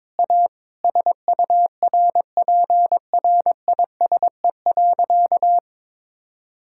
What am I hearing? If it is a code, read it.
Morse code, heard as a keyed tone, about 22 wpm, 700 Hz. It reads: A SURPRISE.
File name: practice / 22wpm / Sentences with 2-Word Phrases - 22wpm